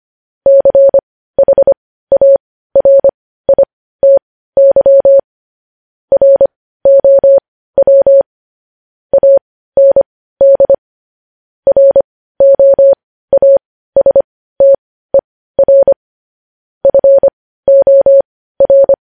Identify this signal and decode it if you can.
Morse code, heard as a keyed tone, about 25 wpm, spaced as for 15 wpm, 550 Hz. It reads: CHARITY ROW AND ROASTER FOR
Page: news-headlines-in-morse-code-at-15-wpm